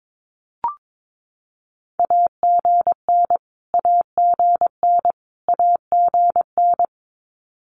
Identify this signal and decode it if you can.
Morse code, heard as a keyed tone, about 22 wpm, 700 Hz. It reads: AGN AGN AGN
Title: Abbreviations Repeated 3x - 22wpm